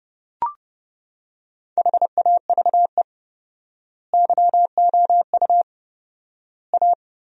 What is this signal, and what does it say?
Morse code, heard as a keyed tone, about 30 wpm, 700 Hz. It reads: HAVE YOU A
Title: 4x Word-Spacing - Sentences from Top 400 Words - 30wpm